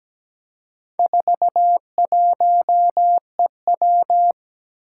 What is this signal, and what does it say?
Morse code, heard as a keyed tone, about 17 wpm, 700 Hz. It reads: 41EW
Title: Four Challenging Letter-Number Combinations 17wpm